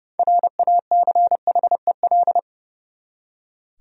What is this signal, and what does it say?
Morse code, heard as a keyed tone, about 30 wpm, 700 Hz. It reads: RACHEL
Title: USA Common Names - 30wpm